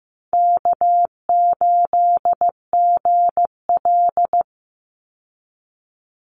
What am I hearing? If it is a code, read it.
Morse code, heard as a keyed tone, about 15 wpm, 700 Hz. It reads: K8GL
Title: Easy Call Signs Repeated 3x - 15wpm